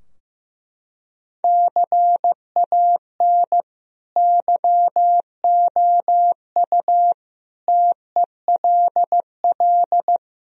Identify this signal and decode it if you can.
Morse code, heard as a keyed tone, about 15 wpm, 700 Hz. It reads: CAN YOU TELL
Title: Sentences from Top 1000 Words - Review Part 2 - 15wpm